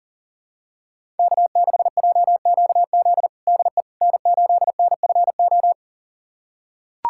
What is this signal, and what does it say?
Morse code, heard as a keyed tone, about 40 wpm, 700 Hz. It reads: K6JQZ DE N8NFO
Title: USA Callsign DE Callsign - 5 Characters - 40wpm